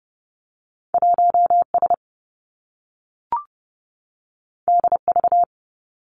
Morse code, heard as a keyed tone, about 30 wpm, 700 Hz. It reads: 1 S D V
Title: Two Challenging Letter-Number Combinations 30wpm